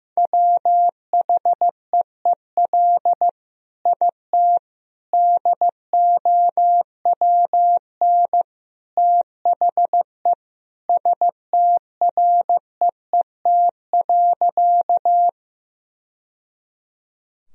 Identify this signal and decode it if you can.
Morse code, heard as a keyed tone, about 15 wpm, 700 Hz. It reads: WHEEL IT DOWN THE STREET.
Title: Sentences from Top 1000 Words - Review Part 3 - 15wpm